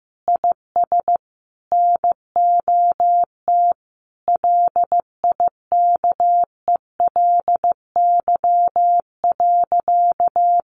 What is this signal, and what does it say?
Morse code, heard as a keyed tone, about 15 wpm, 700 Hz. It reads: IS NOT LIKELY.
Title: Sentences from Top 600 Words 15wpm